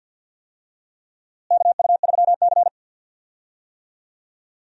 Morse code, heard as a keyed tone, about 52 wpm, 700 Hz. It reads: KU3C
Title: USA Call Signs - 4 Characters - Repeated 3x - 50wpm